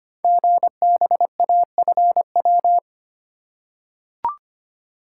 Morse code, heard as a keyed tone, about 25 wpm, 700 Hz. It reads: GBAFW
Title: Code Groups - 5 Letters - 25wpm